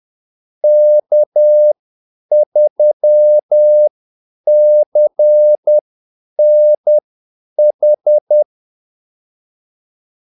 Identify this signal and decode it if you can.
Morse code, heard as a keyed tone, about 10 wpm, spaced as for 8 wpm, 600 Hz. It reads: K3CNH
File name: Callsigns-10-8wpm